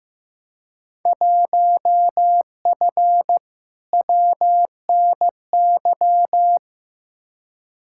Morse code, heard as a keyed tone, about 15 wpm, 700 Hz. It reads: 1F WNY
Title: Field Day Exchanges - 15wpm